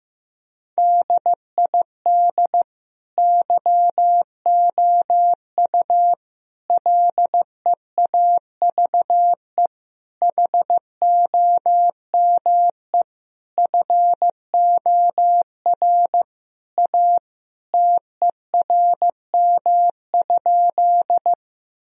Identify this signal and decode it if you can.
Morse code, heard as a keyed tone, about 15 wpm, 700 Hz. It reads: DID YOU LEAVE HOME FOR A TERM?
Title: Sentences from Top 1000 Words - Review Part 4 - 15wpm